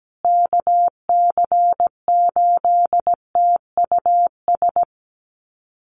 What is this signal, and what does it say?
Morse code, heard as a keyed tone, about 17 wpm, 700 Hz. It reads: KC8TUS
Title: USA Call Signs Repeated 2x 17wpm